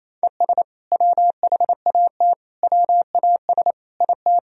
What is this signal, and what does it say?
Morse code, heard as a keyed tone, about 28 wpm, 700 Hz. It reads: ES WHAT WAS IT
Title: Sentences from Top 100 Words and Abbreviations 28wpm